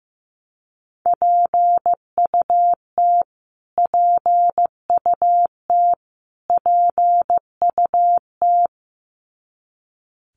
Morse code, heard as a keyed tone, about 15 wpm, 700 Hz. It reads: PUT PUT PUT
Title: Top 300 Words Repeated 15wpm